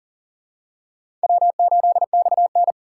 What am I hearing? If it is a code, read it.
Morse code, heard as a keyed tone, about 40 wpm, 700 Hz. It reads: W8XN
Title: USA Call Signs - 4 Characters - 6x Character Spacing - 40wpm